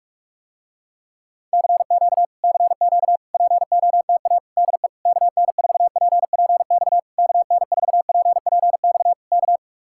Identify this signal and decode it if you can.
Morse code, heard as a keyed tone, about 49 wpm, 700 Hz. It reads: CQ CQ POTA DE KN4PPX KN4PPX K